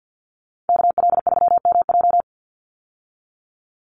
Morse code, heard as a keyed tone, about 50 wpm, 700 Hz. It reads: KL3MJ
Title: USA Call Signs 50wpm